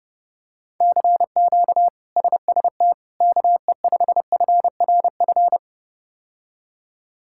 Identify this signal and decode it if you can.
Morse code, heard as a keyed tone, about 30 wpm, 700 Hz. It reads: CQ SST KE5FRF
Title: K1USN SST QSO Practice - 30wpm